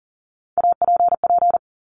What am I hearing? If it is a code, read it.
Morse code, heard as a keyed tone, about 38 wpm, 700 Hz. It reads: APP
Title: Top Three Letter Combinations in Words 40wpm